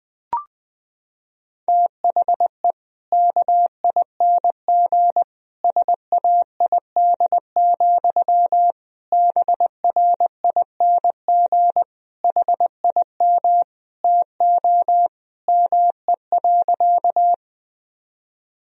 Morse code, heard as a keyed tone, about 21 wpm, 700 Hz. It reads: THE KING SAID, BRING HIM TO ME.
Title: Sentences from Top 1000 Words - Review Part 3 - 20wpm